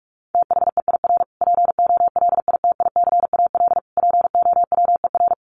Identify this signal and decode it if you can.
Morse code, heard as a keyed tone, about 44 wpm, 700 Hz. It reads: THEIR POLITICAL POWER
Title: Sentences from Top 1300 Words 45wpm